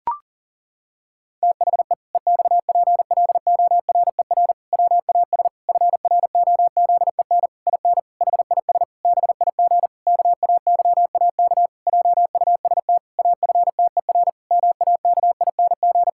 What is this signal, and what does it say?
Morse code, heard as a keyed tone, about 40 wpm, 700 Hz. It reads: THE EXPLORER WAS FROZEN IN HIS BIG KAYAK JUST AFTER MAKING
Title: Pangram Challenge v2 - 40wpm